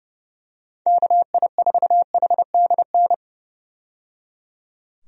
Morse code, heard as a keyed tone, about 30 wpm, 700 Hz. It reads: KI4HDN
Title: USA Call Signs - 6 Characters - 30wpm